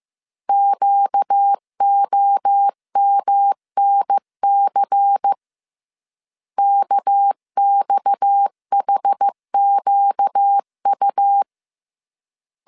Morse code, heard as a keyed tone, about 15 wpm, 800 Hz. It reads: QOMNC KXHQU
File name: CW / DET / 1 6 1500C S A 14wpm 2921